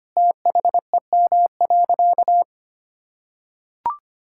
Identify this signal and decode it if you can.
Morse code, heard as a keyed tone, about 25 wpm, 700 Hz. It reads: THEM.
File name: practice / 25wpm / Difficult Sentences with Hints - 25wpm